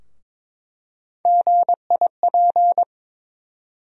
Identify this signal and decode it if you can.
Morse code, heard as a keyed tone, about 21 wpm, 700 Hz. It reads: GIP
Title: USA Call Sign Postfixes - 3 Letters 22wpm